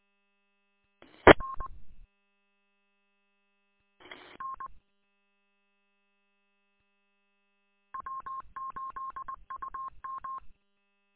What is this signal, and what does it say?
Morse code, heard as a keyed tone, about 20 wpm, 1070 Hz. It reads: R N W8UM
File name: w8um- -combined